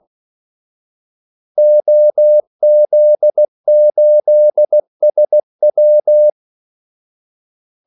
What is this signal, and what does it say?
Morse code, heard as a keyed tone, about 16 wpm, 600 Hz. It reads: OZ8SW
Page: Kaldesignaler 16 wpm